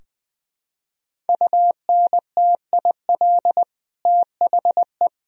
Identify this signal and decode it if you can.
Morse code, heard as a keyed tone, about 21 wpm, 700 Hz. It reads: UNTIL THE
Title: Sentences from Top 1000 Words - Review Part 3 - 20wpm